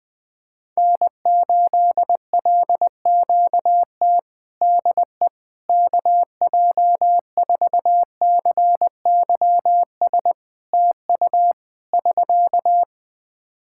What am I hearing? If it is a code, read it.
Morse code, heard as a keyed tone, about 20 wpm, 700 Hz. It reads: N8LQT DE KJ4CYS TU <SK>
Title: K1USN SST QSO Practice 20wpm